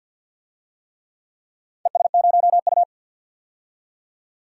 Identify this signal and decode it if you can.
Morse code, heard as a keyed tone, about 49 wpm, 700 Hz. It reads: ES0U
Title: DXCC-100 Call Signs - 4 Characters - 50wpm